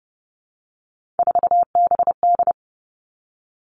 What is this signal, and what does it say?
Morse code, heard as a keyed tone, about 32 wpm, 700 Hz. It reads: 4BD
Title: Three Challenging Letter-Number Combinations 30wpm